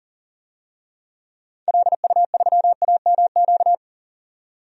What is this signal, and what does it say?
Morse code, heard as a keyed tone, about 40 wpm, 700 Hz. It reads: LU3AMQ
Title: DXCC-100 Call Signs - 6 Characters - 40wpm